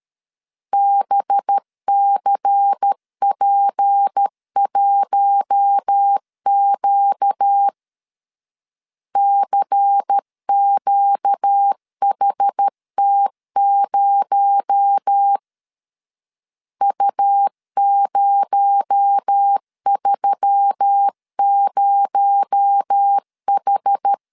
Morse code, heard as a keyed tone, about 13 wpm, 800 Hz. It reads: BCP1Q CQHT0 U030H
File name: LEZIONI / 6 PKX0 / 6 PKX0 12wpm 2231